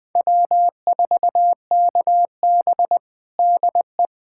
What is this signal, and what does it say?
Morse code, heard as a keyed tone, about 20 wpm, 700 Hz. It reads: W4KB DE
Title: USA Callsign DE Callsign - 4 Characters - 20wpm